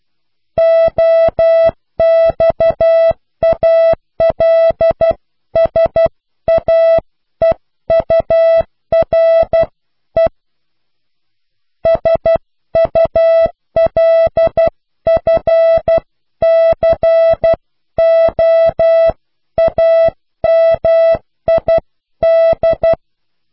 Morse code, heard as a keyed tone, about 12 wpm, 660 Hz. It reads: OXALSAEURE SULFCOAMID
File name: cw / pseudo worden(10)